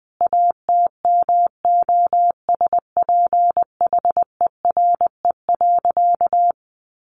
Morse code, heard as a keyed tone, about 20 wpm, 700 Hz. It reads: ATMOSPHERE.
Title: A Sound Of Thunder 20wpm